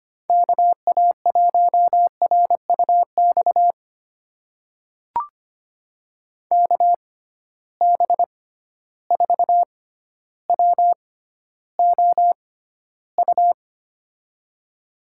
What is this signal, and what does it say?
Morse code, heard as a keyed tone, about 25 wpm, spaced as for 13 wpm, 700 Hz. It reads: KA1RUX KB4WOU
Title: USA Call Signs - 6 Characters - 6x Character Spacing - 25wpm